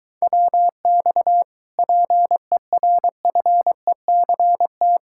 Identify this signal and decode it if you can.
Morse code, heard as a keyed tone, about 23 wpm, 700 Hz. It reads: WX PERFECT